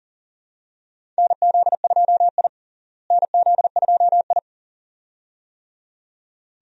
Morse code, heard as a keyed tone, about 40 wpm, 700 Hz. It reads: NZ2I NZ2I
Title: USA Call Signs - 4 Characters - Repeated 2x - 40wpm